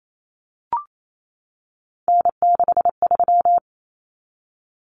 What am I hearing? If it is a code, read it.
Morse code, heard as a keyed tone, about 28 wpm, 700 Hz. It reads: N63
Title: Common Cut Numbers - 3-Digit 28wpm